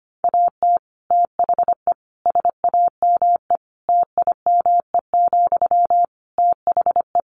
Morse code, heard as a keyed tone, about 25 wpm, 700 Hz. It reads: AT THE SAME TIME, THE
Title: Sentences from Top 1500 Words 25wpm